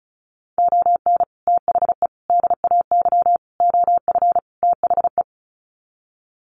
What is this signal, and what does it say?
Morse code, heard as a keyed tone, about 34 wpm, 700 Hz. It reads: ON THE DAY OF THE
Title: Sets of 5 Words Encore 35wpm